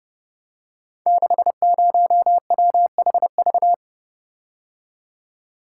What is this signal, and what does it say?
Morse code, heard as a keyed tone, about 30 wpm, 700 Hz. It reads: 60WHV
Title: Code Groups - 5 Letter:Numbers - 30wpm